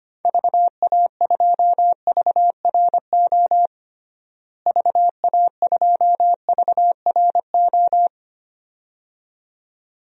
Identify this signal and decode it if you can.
Morse code, heard as a keyed tone, about 25 wpm, 700 Hz. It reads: VA2VRO VA2VRO
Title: DXCC-100 Call Signs - 6 Characters - Repeated 2x - 25wpm